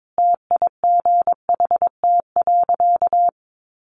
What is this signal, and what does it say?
Morse code, heard as a keyed tone, about 22 wpm, 700 Hz. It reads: TIGHT.
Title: A Sound Of Thunder 22wpm